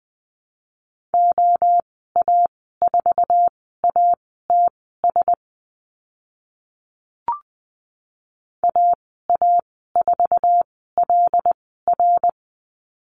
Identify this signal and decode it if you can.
Morse code, heard as a keyed tone, about 20 wpm, spaced as for 14 wpm, 700 Hz. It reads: OA4ATS AA4LR
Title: 082 Lesson 14g - Number 2 - Call Signs 20wpm